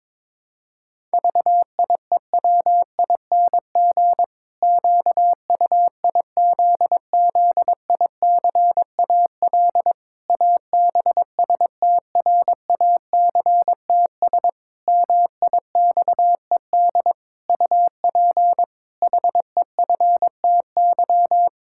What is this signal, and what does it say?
Morse code, heard as a keyed tone, about 22 wpm, 700 Hz. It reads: VIEWING QUIZZICAL ABSTRACTS MIXED UP HEFTY
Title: Pangram Challenge v2 - 22wpm